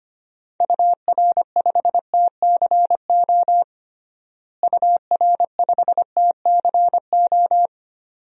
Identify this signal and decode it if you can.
Morse code, heard as a keyed tone, about 25 wpm, 700 Hz. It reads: UR5TCO UR5TCO
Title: DXCC-100 Call Signs - 6 Characters - Repeated 2x - 25wpm